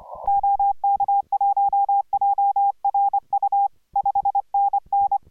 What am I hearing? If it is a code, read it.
Morse code, heard as a keyed tone, about 25 wpm, 810 Hz. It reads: OK1JRU 5NN